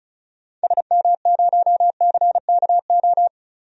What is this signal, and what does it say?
Morse code, heard as a keyed tone, about 36 wpm, 700 Hz. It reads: SM0CKO
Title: DXCC-100 Call Signs - 6 Characters - 35wpm